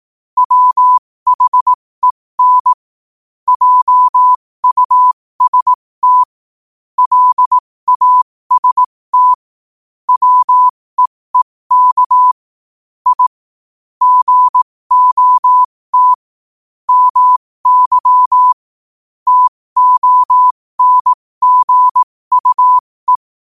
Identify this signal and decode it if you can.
Morse code, heard as a keyed tone, about 18 wpm, spaced as for 15 wpm, 1000 Hz. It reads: WHEN JUST LAST WEEK I GOT MY TONGUE
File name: qotdfiles / 15wpm